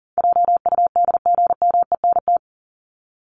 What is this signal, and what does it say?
Morse code, heard as a keyed tone, about 40 wpm, 700 Hz. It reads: JUDGMENT
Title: Top 1751-2000 Words 40wpm